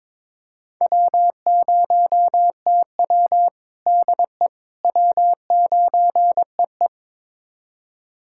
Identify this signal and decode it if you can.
Morse code, heard as a keyed tone, about 22 wpm, 700 Hz. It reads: W0TW DE W9EE
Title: USA Callsign DE Callsign - 4 Characters - 22wpm